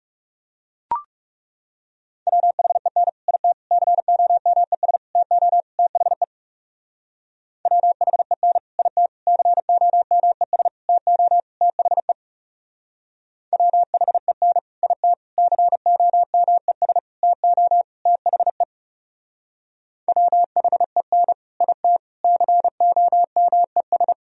Morse code, heard as a keyed tone, about 38 wpm, 700 Hz. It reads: WHEN IT COMES TO THE WHEN IT COMES TO THE WHEN IT COMES TO THE WHEN IT COMES